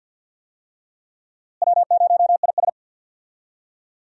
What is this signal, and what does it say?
Morse code, heard as a keyed tone, about 46 wpm, 700 Hz. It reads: W0IS
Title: USA Call Signs - 4 Characters - Repeated 3x - 50wpm